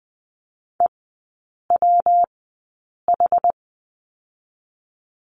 Morse code, heard as a keyed tone, about 20 wpm, spaced as for 8 wpm, 700 Hz. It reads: EWH
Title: 138 Lesson 23e - Letter f - Three Characters 20wpm